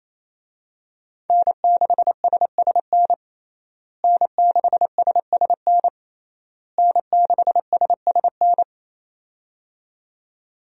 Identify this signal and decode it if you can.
Morse code, heard as a keyed tone, about 28 wpm, 700 Hz. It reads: N6SSN N6SSN N6SSN
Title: USA Call Signs - 5 Characters - Repeated 3x - 28wpm